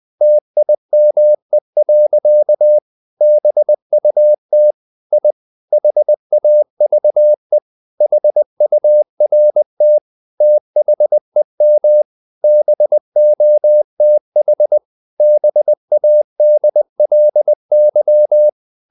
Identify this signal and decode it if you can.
Morse code, heard as a keyed tone, about 20 wpm, 600 Hz. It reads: TIME. BUT I HAVE HURT THEM BOTH BADLY
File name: the old man and the sea part 2 20wpm